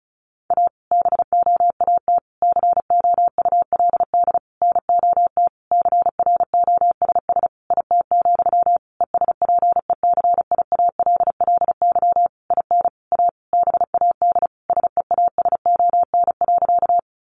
Morse code, heard as a keyed tone, about 34 wpm, 700 Hz. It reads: A BOAT COULD NOT CROSS IT, ESPECIALLY IN A BAD SEASON.
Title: Sentences from Top 1400 Words 35wpm